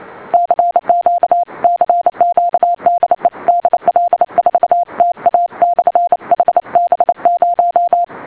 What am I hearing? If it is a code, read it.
Morse code, heard as a keyed tone, about 28 wpm, 700 Hz. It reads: CQ CQ DE DL4TA/HB0